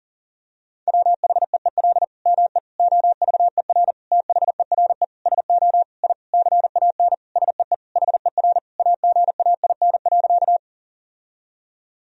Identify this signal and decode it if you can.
Morse code, heard as a keyed tone, about 40 wpm, 700 Hz. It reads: WHEEL ME OVER THERE SO I CAN SEE HER AGAIN.
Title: Sentences from Top 1000 Words - Review Part 3 - 40wpm